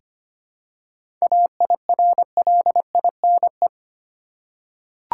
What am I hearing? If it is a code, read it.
Morse code, heard as a keyed tone, about 25 wpm, 700 Hz. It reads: AIRLINE
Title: Seven Letter Words - 25wpm